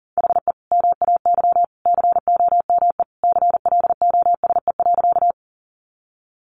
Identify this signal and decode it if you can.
Morse code, heard as a keyed tone, about 40 wpm, 700 Hz. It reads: HE MAY COME CLOSE.
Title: Sentences from Top 300 Words 40wpm